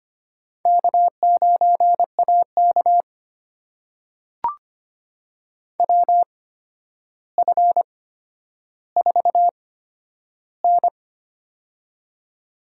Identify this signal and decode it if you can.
Morse code, heard as a keyed tone, about 25 wpm, spaced as for 11 wpm, 700 Hz. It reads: K9AK WF4N
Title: USA Call Signs - 4 Characters - 8x Character Spacing - 25wpm